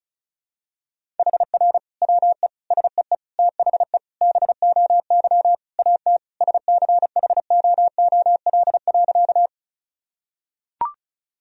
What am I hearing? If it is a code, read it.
Morse code, heard as a keyed tone, about 34 wpm, 700 Hz. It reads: HR WE SEE THE BOY AT SCHOOL.
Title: Sentences from Top 300 Words and Abbreviations 35wpm